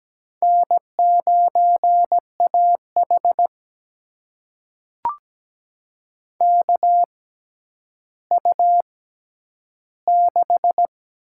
Morse code, heard as a keyed tone, about 17 wpm, spaced as for 10 wpm, 700 Hz. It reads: N9AH KU6
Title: USA Call Signs - 4 Characters - 6x Character Spacing - 17wpm